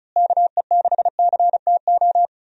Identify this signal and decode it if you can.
Morse code, heard as a keyed tone, about 35 wpm, 700 Hz. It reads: KE6CTO
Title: USA Call Signs - 6 Characters - Repeated 2x - 35wpm